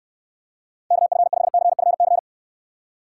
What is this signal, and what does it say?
Morse code, heard as a keyed tone, about 65 wpm, 700 Hz. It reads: KF5CFC